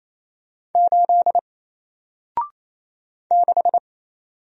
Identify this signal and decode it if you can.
Morse code, heard as a keyed tone, about 27 wpm, 700 Hz. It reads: 8 6
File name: Single-Letter-Number-28wpm